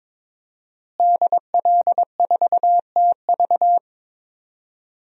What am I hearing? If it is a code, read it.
Morse code, heard as a keyed tone, about 22 wpm, 700 Hz. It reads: DL4TV
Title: DXCC-100 Call Signs - 5 Characters - Repeated 3x - 22wpm